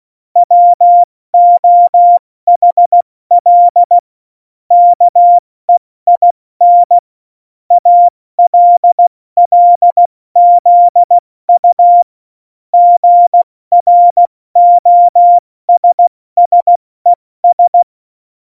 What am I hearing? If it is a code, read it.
Morse code, heard as a keyed tone, about 16 wpm, 700 Hz. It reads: WOHL KEIN ALLZU GROSSES
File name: Morsekurs / Witze / Witz 08. September 2020, 70 F80